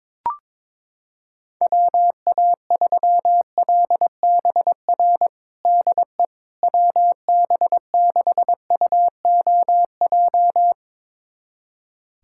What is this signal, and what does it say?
Morse code, heard as a keyed tone, about 22 wpm, 700 Hz. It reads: WA3LBR DE WB6UOJ
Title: USA Callsign DE Callsign - 6 Characters - 22wpm